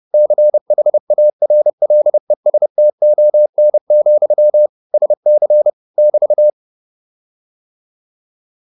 Morse code, heard as a keyed tone, about 30 wpm, 600 Hz. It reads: CHARLESTON, SC <BT>